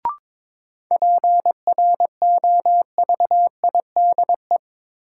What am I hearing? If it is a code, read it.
Morse code, heard as a keyed tone, about 22 wpm, 700 Hz. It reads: PROVIDE
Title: Sentences from Top 1000 Words - Review Part 2 - New Word Drill - 22wpm